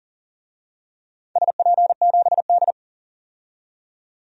Morse code, heard as a keyed tone, about 39 wpm, 700 Hz. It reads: SP7D
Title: DXCC-100 Call Signs - 4 Characters - 40wpm